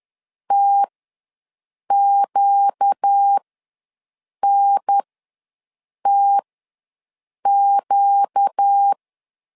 Morse code, heard as a keyed tone, about 10 wpm, spaced as for 6 wpm, 800 Hz. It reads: TQNTQ
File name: LEZIONI / 4 Q ws 10wpm 2306